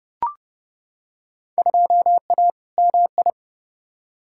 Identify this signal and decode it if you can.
Morse code, heard as a keyed tone, about 30 wpm, 700 Hz. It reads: 2A MI
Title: Field Day Exchanges - 30wpm